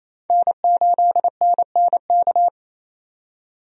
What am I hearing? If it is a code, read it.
Morse code, heard as a keyed tone, about 28 wpm, 700 Hz. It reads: N8NNK
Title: USA Call Signs - 5 Characters - 28wpm